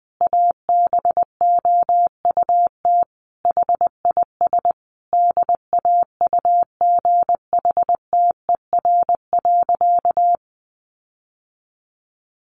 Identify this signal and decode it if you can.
Morse code, heard as a keyed tone, about 20 wpm, 700 Hz. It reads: ABOUT HIS DAUGHTER.
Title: Sentences from Top 1500 Words 20wpm